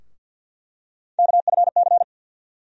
Morse code, heard as a keyed tone, about 48 wpm, 700 Hz. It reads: KFC
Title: USA Call Sign Postfixes - 3 Letters 50wpm